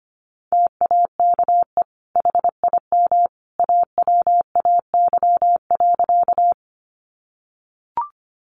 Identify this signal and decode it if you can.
Morse code, heard as a keyed tone, about 25 wpm, 700 Hz. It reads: TAKE HIM AWAY.
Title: Sentences from Top 600 Words 25wpm